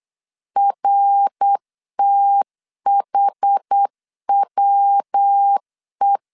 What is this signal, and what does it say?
Morse code, heard as a keyed tone, about 8 wpm, 800 Hz. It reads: RTHWE